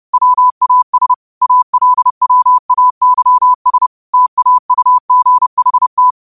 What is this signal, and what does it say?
Morse code, heard as a keyed tone, about 30 wpm, 1000 Hz. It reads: WAS ALWAYS TAUGHT